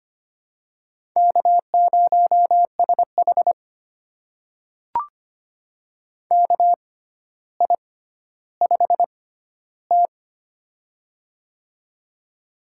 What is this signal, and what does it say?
Morse code, heard as a keyed tone, about 25 wpm, spaced as for 13 wpm, 700 Hz. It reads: K0SH KI5T
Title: USA Call Signs - 4 Characters - 6x Character Spacing - 25wpm